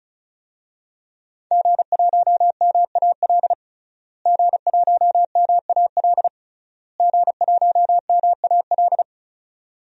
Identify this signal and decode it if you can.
Morse code, heard as a keyed tone, about 36 wpm, 700 Hz. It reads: G1MAL G1MAL G1MAL
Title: DXCC-100 Call Signs - 5 Characters - Repeated 3x - 35wpm